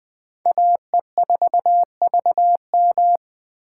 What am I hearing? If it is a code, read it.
Morse code, heard as a keyed tone, about 20 wpm, 700 Hz. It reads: AE4VM
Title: Speed Racing - USA 2x2 Call Signs 17wpm